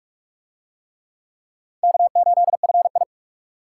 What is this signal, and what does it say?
Morse code, heard as a keyed tone, about 42 wpm, 700 Hz. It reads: K8FI